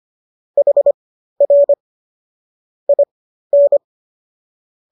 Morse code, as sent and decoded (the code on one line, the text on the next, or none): .... .-. / .. -.
HR IN